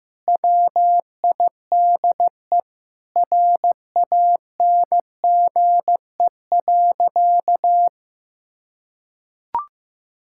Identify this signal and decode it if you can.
Morse code, heard as a keyed tone, about 16 wpm, 700 Hz. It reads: WIDE RANGE.
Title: Sentences from Top 1000 Words - Review Part 6 - 15wpm